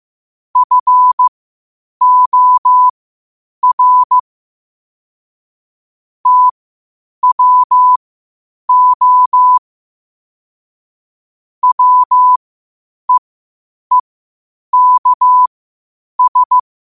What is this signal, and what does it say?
Morse code, heard as a keyed tone, about 15 wpm, spaced as for 8 wpm, 1000 Hz. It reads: FOR TWO WEEKS
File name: qotdfiles / 8wpm